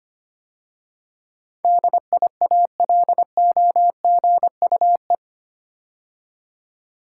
Morse code, heard as a keyed tone, about 25 wpm, 700 Hz. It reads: DIALOGUE
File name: 8 Letter Words - First 3 Letters Given - 25wpm